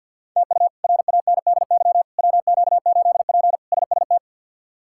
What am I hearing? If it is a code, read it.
Morse code, heard as a keyed tone, about 52 wpm, 700 Hz. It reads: TU RANDY WQ8P SST
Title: K1USN SST QSO Practice - 50wpm